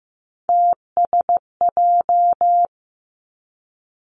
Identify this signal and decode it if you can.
Morse code, heard as a keyed tone, about 15 wpm, 700 Hz. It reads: TSJ
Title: Three Challenging Letter-Number Combinations 15wpm